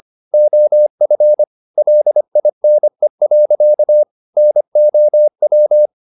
Morse code, heard as a keyed tone, about 25 wpm, 600 Hz. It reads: OF LINE. NOW
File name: the old man and the sea part 1 25wpm